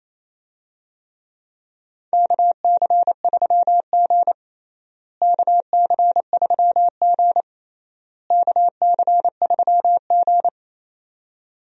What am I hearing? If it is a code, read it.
Morse code, heard as a keyed tone, about 28 wpm, 700 Hz. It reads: KC3G KC3G KC3G
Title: USA Call Signs - 4 Characters - Repeated 3x - 28wpm